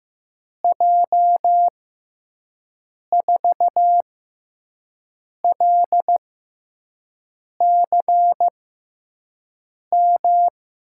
Morse code, heard as a keyed tone, about 15 wpm, spaced as for 5 wpm, 700 Hz. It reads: J4LCM
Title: USA Call Signs - 6 Characters - 6x Character Spacing - 15wpm